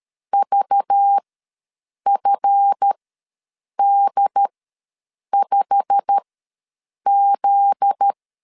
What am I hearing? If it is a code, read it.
Morse code, heard as a keyed tone, about 13 wpm, spaced as for 7 wpm, 800 Hz. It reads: VFD5Z
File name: LEZIONI / 8 YJ45 / 8 YJ45 ws 12wpm 3146